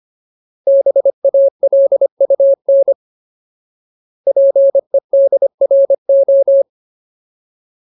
Morse code, heard as a keyed tone, about 25 wpm, 550 Hz. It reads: BALUN PEDRO
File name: Ultimate Word List / 25WPM Ultimate